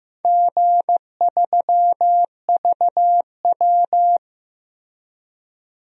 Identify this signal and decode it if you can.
Morse code, heard as a keyed tone, about 15 wpm, 700 Hz. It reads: G3VW
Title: DXCC-100 Call Signs - 4 Characters - Repeated 2x - 15wpm